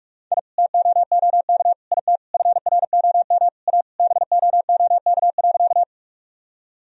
Morse code, heard as a keyed tone, about 46 wpm, 700 Hz. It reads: I TOOK IT FROM A BOOK.
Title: Sentences from Top 1000 Words - Review Part 1 - 45wpm